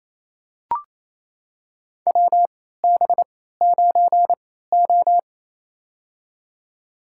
Morse code, heard as a keyed tone, about 28 wpm, spaced as for 16 wpm, 700 Hz. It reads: WB9O
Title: USA Call Signs - 3x Character Spacing - 28wpm